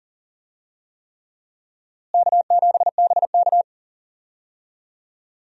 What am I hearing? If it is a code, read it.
Morse code, heard as a keyed tone, about 40 wpm, 700 Hz. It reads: K7BK
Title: USA Call Signs - 4 Characters - Repeated 3x - 40wpm